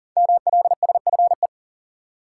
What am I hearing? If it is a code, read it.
Morse code, heard as a keyed tone, about 40 wpm, 700 Hz. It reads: MLSFE
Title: Code Groups - 5 Letters - 40wpm